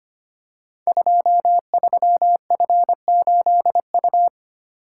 Morse code, heard as a keyed tone, about 25 wpm, 700 Hz. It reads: 23F8U
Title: Code Groups - 5 Letter:Numbers - 25wpm